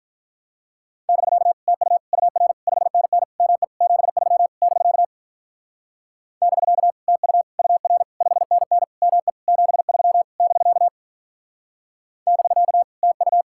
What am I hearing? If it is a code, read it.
Morse code, heard as a keyed tone, about 48 wpm, 700 Hz. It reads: <BK> TU UR 5NN ME 73 <BK> <BK> TU UR 5NN ME 73 <BK> <BK> TU